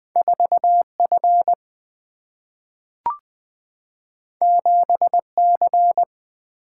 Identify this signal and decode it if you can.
Morse code, heard as a keyed tone, about 20 wpm, 700 Hz. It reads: 4 F 7 C
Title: Two Letter-Number Combinations - 20wpm